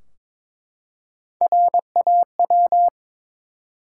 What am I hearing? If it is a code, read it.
Morse code, heard as a keyed tone, about 22 wpm, 700 Hz. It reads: RAW
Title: Three Letter Words - 22wpm